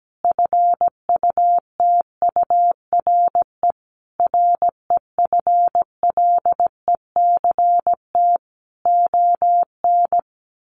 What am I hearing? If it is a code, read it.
Morse code, heard as a keyed tone, about 17 wpm, 700 Hz. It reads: FUTURE REFLECT ON
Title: Sentences from Top 1500 Words 17wpm